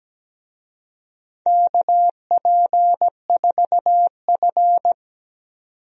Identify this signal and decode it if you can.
Morse code, heard as a keyed tone, about 17 wpm, 700 Hz. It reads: KP4F
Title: DXCC-100 Call Signs - 4 Characters - 17wpm